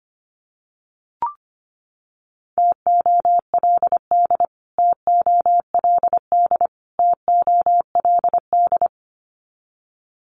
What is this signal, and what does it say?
Morse code, heard as a keyed tone, about 25 wpm, 700 Hz. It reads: TOLD TOLD TOLD
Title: Top 500 Words Repeated 25wpm